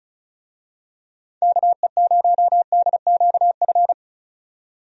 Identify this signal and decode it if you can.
Morse code, heard as a keyed tone, about 36 wpm, 700 Hz. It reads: KE0DQF
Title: USA Call Signs - 6 Characters - 35wpm